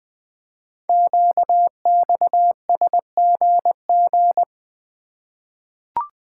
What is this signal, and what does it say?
Morse code, heard as a keyed tone, about 20 wpm, 700 Hz. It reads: QXSGG
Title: Code Groups - 5 Letter:Numbers - 20wpm